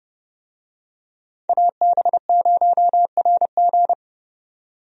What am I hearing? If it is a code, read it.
Morse code, heard as a keyed tone, about 31 wpm, 700 Hz. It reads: AB0RG
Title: USA Call Signs - 5 Characters - 30wpm